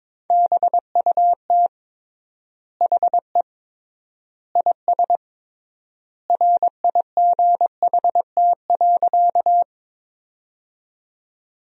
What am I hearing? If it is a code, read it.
Morse code, heard as a keyed tone, about 21 wpm, 700 Hz. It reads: BUT HE IS RIGHT.
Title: 3x Word-Spacing - Sentences from Top 200 Words - 22wpm